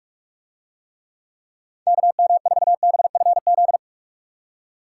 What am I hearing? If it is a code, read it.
Morse code, heard as a keyed tone, about 45 wpm, 700 Hz. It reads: KM4BFZ